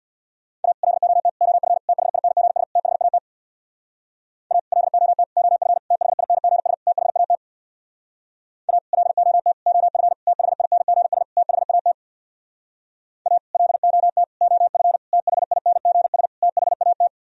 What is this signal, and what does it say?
Morse code, heard as a keyed tone, about 63 wpm, 700 Hz. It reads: A LOT OF THINGS THAT A LOT OF THINGS THAT A LOT OF THINGS THAT A LOT OF THINGS THAT